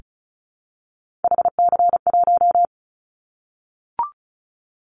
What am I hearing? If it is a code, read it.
Morse code, heard as a keyed tone, about 34 wpm, 700 Hz. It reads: HC1
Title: Callsign Prefixes for Top 100 DXCC - 35wpm